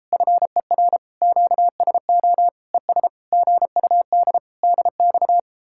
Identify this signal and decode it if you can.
Morse code, heard as a keyed tone, about 33 wpm, 700 Hz. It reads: FER QSO ES GUD DX